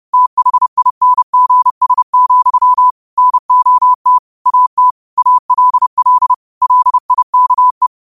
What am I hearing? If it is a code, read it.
Morse code, heard as a keyed tone, about 30 wpm, 1000 Hz. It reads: THINGS, NOT AT ALL LIKE
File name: qotdfiles / 30wpm